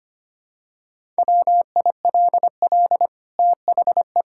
Morse code, heard as a keyed tone, about 25 wpm, 700 Hz. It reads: WILL THE
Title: Sentences from Top 1000 Words - Review Part 6 - 25wpm